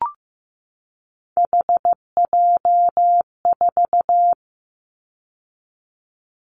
Morse code, heard as a keyed tone, about 15 wpm, 700 Hz. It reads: HJ4
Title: Callsign Prefixes for Top 100 DXCC - 15wpm